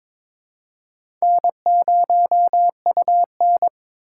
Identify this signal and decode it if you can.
Morse code, heard as a keyed tone, about 22 wpm, 700 Hz. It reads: N0UN
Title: USA Call Signs - 4 Characters - 2x Character Spacing - 22wpm